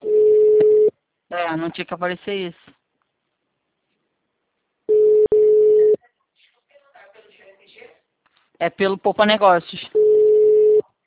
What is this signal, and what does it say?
Ringback tone (Europe, 425 Hz), with speech.